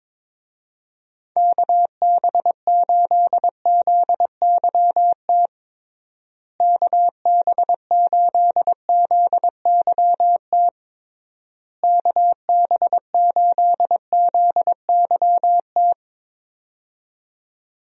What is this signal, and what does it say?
Morse code, heard as a keyed tone, about 22 wpm, 700 Hz. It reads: KB8ZYT KB8ZYT KB8ZYT
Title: USA Call Signs - 6 Characters - Repeated 3x - 22wpm